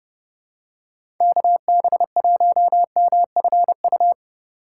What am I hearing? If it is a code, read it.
Morse code, heard as a keyed tone, about 30 wpm, 700 Hz. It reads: KB1MFU
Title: USA Call Signs - 6x Character Spacing - 30wpm